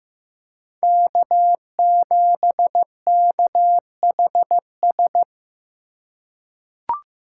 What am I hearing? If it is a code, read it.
Morse code, heard as a keyed tone, about 15 wpm, 700 Hz. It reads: K7KHS
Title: USA Call Signs - 2x Character Spacing - 15wpm